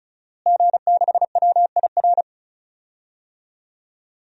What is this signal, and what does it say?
Morse code, heard as a keyed tone, about 36 wpm, 700 Hz. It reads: G6WIR
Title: DXCC-100 Call Signs - 5 Characters - Repeated 2x - 35wpm